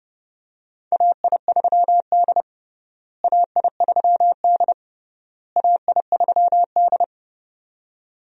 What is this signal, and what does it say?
Morse code, heard as a keyed tone, about 30 wpm, 700 Hz. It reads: AI3D AI3D AI3D
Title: USA Call Signs - 4 Characters - Repeated 3x - 30wpm